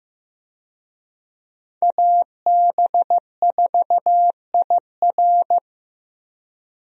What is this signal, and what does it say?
Morse code, heard as a keyed tone, about 16 wpm, 700 Hz. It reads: AB4IR
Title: USA Call Signs - 5 Characters - Repeated 2x - 15wpm